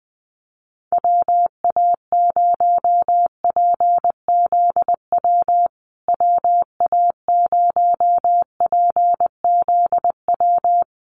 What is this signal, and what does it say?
Morse code, heard as a keyed tone, about 20 wpm, 700 Hz. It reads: WA0PZW WA0PZW
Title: USA Call Signs Repeated 3x 20wpm